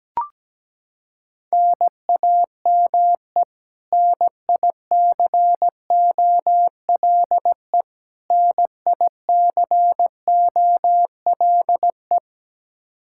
Morse code, heard as a keyed tone, about 17 wpm, 700 Hz. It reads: NAME NICOLE NICOLE
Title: Practice QSO - Exchange 1 - Beginning - 17wpm